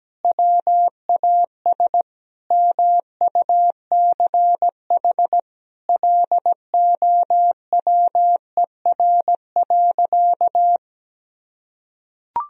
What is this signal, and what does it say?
Morse code, heard as a keyed tone, about 17 wpm, 700 Hz. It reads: WAS MUCH LOWER.
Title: Sentences from Top 1000 Words - Review Part 4 - 17wpm